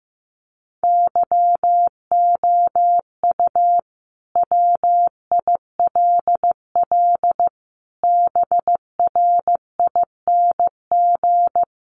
Morse code, heard as a keyed tone, about 15 wpm, 700 Hz. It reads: YOU WILL BRING
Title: Sentences from Top 1300 Words 15wpm